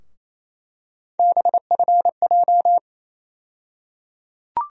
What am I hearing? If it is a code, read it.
Morse code, heard as a keyed tone, about 28 wpm, 700 Hz. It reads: BFJ
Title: USA Call Sign Postfixes - 3 Letters 28wpm